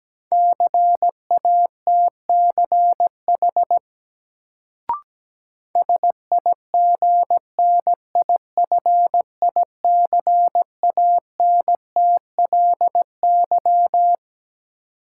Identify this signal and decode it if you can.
Morse code, heard as a keyed tone, about 17 wpm, 700 Hz. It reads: CATCH SIGNIFICANTLY
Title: Sentences from Top 1500 Words - New Word Drill - 17wpm